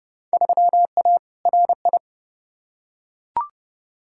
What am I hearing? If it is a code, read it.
Morse code, heard as a keyed tone, about 30 wpm, 700 Hz. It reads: 3A RI
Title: Field Day Exchanges - 30wpm